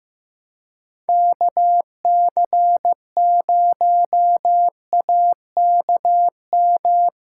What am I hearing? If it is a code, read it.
Morse code, heard as a keyed tone, about 15 wpm, 700 Hz. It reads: KC0AKM
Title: USA Callsign DE Callsign 15wpm